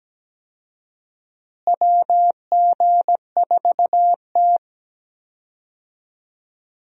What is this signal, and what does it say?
Morse code, heard as a keyed tone, about 17 wpm, 700 Hz. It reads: WG4T
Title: USA Call Signs - 4 Characters - Repeated 2x - 17wpm